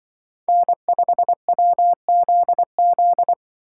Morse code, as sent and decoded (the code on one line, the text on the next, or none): -. ..... .-- --.. --..
N5WZZ